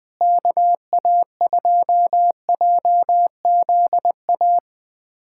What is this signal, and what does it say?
Morse code, heard as a keyed tone, about 20 wpm, 700 Hz. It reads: KA2JZA
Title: USA Callsign DE Callsign 20wpm